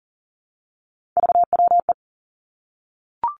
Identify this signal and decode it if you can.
Morse code, heard as a keyed tone, about 37 wpm, 700 Hz. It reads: VWE
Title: Three Challenging Letter-Number Combinations 40wpm